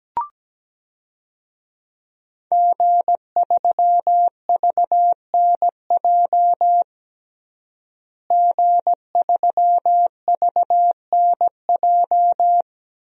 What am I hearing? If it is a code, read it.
Morse code, heard as a keyed tone, about 17 wpm, 700 Hz. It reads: G3VNJ G3VNJ
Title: DXCC-100 Call Signs - 5 Characters - Repeated 3x - 17wpm